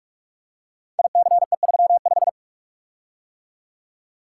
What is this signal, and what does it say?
Morse code, heard as a keyed tone, about 44 wpm, 700 Hz. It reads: ICE35
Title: Code Groups - 5 Letter:Numbers - 45wpm